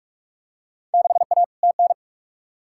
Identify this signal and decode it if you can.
Morse code, heard as a keyed tone, about 41 wpm, 700 Hz. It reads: 6A TN
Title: Field Day Exchanges Encore - 45wpm